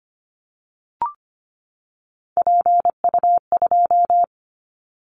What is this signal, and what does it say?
Morse code, heard as a keyed tone, about 25 wpm, 690 Hz. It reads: PU2
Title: Callsign Prefixes for Top 50 DXCC - 25wpm